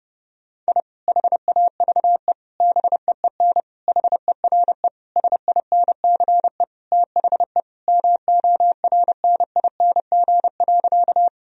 Morse code, heard as a keyed tone, about 32 wpm, 700 Hz. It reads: I HAVE BEEN HERE SINCE THE MORNING.
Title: Sentences from Top 1000 Words - Review Part 1 - 30wpm